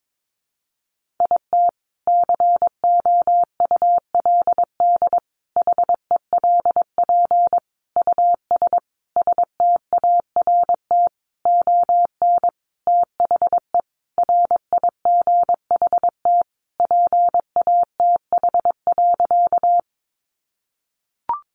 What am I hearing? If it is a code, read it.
Morse code, heard as a keyed tone, about 22 wpm, 700 Hz. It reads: IT COULD HELP US START ON THE RIGHT PATH.
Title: Sentences from Top 1400 Words 22wpm